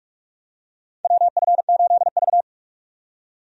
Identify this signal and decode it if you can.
Morse code, heard as a keyed tone, about 46 wpm, 700 Hz. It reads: WF8V
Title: USA Call Signs - 4 Characters - 45wpm